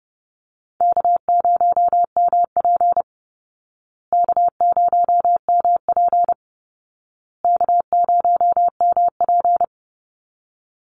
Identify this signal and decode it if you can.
Morse code, heard as a keyed tone, about 30 wpm, 700 Hz. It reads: K0MP K0MP K0MP
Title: Call Signs Repeated 2x - 30wpm